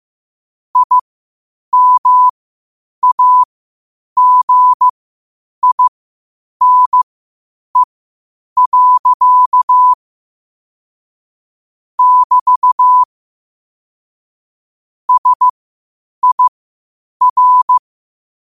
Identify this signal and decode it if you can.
Morse code, heard as a keyed tone, about 15 wpm, spaced as for 8 wpm, 1000 Hz. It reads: IMAGINE. <BT> SIR
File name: qotdfiles / 8wpm